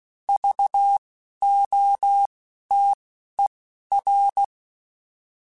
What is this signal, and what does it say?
Morse code, heard as a keyed tone, about 16 wpm, spaced as for 11 wpm, 800 Hz. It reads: VOTER
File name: Alphabetical / Alphabetical mp3s / Vs @10